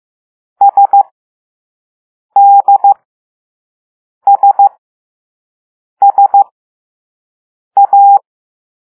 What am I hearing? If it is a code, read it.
Morse code, heard as a keyed tone, about 15 wpm, spaced as for 5 wpm, 800 Hz. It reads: SDSSA
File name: Morse / Level3B